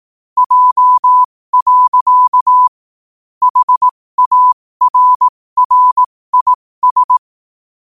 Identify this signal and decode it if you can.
Morse code, heard as a keyed tone, about 18 wpm, spaced as for 15 wpm, 1000 Hz. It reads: J. HARRIS
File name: qotdfiles / 15wpm